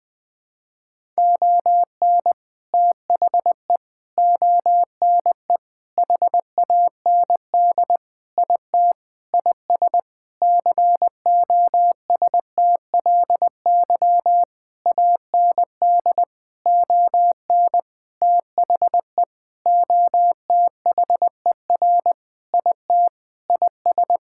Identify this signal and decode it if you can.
Morse code, heard as a keyed tone, about 20 wpm, 700 Hz. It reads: ON THE ONE HAND IT IS COSTLY AND ON THE OTHER IT IS
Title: Sentences with 8-Word Phrases - 20wpm